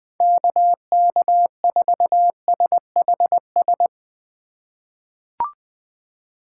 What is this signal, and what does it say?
Morse code, heard as a keyed tone, about 20 wpm, 700 Hz. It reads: KK4SHS
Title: USA Call Signs - 6 Characters - 4x Character Spacing - 20wpm